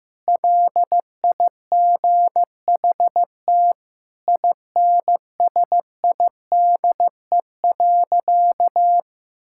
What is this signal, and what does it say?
Morse code, heard as a keyed tone, about 14 wpm, 700 Hz. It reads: LIGHT INSIDE.
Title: Sentences from Top 1000 Words - Review Part 6 - 15wpm